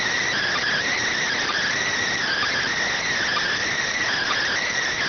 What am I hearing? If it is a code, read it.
Emergency Alert System (SAME) tones, heard as harsh buzzing bursts. It is heard from 0.6 to 4.3 s.